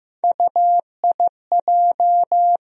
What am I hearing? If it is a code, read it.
Morse code, heard as a keyed tone, about 15 wpm, 700 Hz. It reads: UIJ
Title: USA Call Sign Postfixes - 3 Letters 15wpm